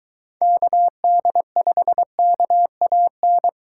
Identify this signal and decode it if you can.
Morse code, heard as a keyed tone, about 22 wpm, 700 Hz. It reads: KD5KAN